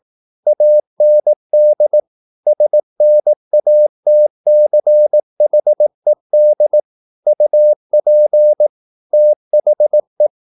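Morse code, heard as a keyed tone, about 18 wpm, 600 Hz. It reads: AND SNATCHED UP THE
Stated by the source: Morse practice at 18 wpm, English text, from CW med Gnister.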